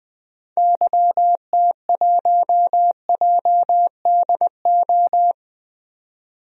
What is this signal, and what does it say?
Morse code, heard as a keyed tone, about 21 wpm, 700 Hz. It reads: YT1JDO
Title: DXCC-100 Call Signs - 6 Characters - 20wpm